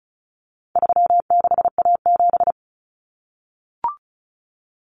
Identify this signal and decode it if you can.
Morse code, heard as a keyed tone, about 34 wpm, 700 Hz. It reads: 36A7
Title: Common Cut Numbers - 4-Digit 35wpm